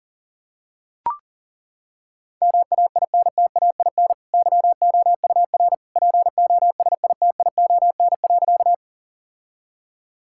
Morse code, heard as a keyed tone, about 37 wpm, 700 Hz. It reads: MAINTAIN YOUR POSITION.
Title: Sentences from Top 1000 Words - Review Part 5 - 40wpm